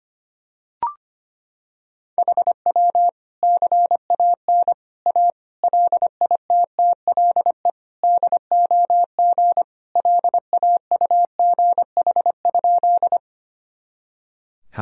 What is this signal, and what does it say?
Morse code, heard as a keyed tone, about 25 wpm, 700 Hz. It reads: HW CAN A LITTLE DOG LAUGH?
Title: Sentences from Top 500 Words and Abbreviations 25wpm